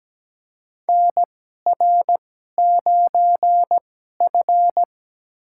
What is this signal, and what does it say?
Morse code, heard as a keyed tone, about 17 wpm, spaced as for 12 wpm, 700 Hz. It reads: NR9F
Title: USA Call Signs - 4 Characters - 2x Character Spacing - 17wpm